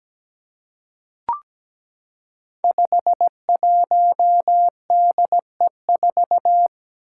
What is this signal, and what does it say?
Morse code, heard as a keyed tone, about 17 wpm, 700 Hz. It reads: 51DE4
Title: Five Challenging Letter-Number Combinations 17wpm